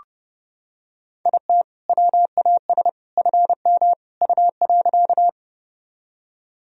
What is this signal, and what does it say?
Morse code, heard as a keyed tone, about 30 wpm, 700 Hz. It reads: IT WAS FM U.
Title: Sentences from Top 100 Words and Abbreviations 30wpm